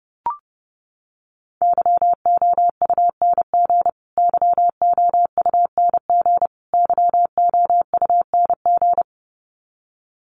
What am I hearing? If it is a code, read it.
Morse code, heard as a keyed tone, about 30 wpm, 700 Hz. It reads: YOUNG YOUNG YOUNG
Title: Top 500 Words Repeated 30wpm